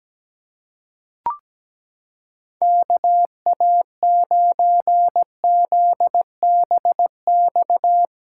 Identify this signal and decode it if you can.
Morse code, heard as a keyed tone, about 17 wpm, 700 Hz. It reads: KA9ZBX
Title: USA Callsign DE Callsign - 6 Characters - 17wpm